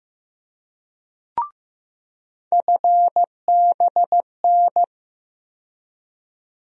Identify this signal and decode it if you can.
Morse code, heard as a keyed tone, about 15 wpm, 700 Hz. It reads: FBN
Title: USA Call Sign Postfixes - 3 Letters 15wpm